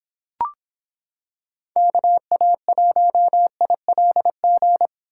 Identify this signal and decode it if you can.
Morse code, heard as a keyed tone, about 26 wpm, 700 Hz. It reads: KA1ILG